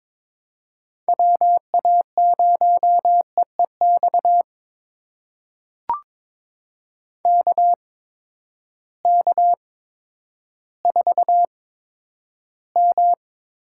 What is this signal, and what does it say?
Morse code, heard as a keyed tone, about 22 wpm, spaced as for 11 wpm, 700 Hz. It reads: WA0EEX KK4M
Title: USA Call Signs - 6 Characters - 8x Character Spacing - 22wpm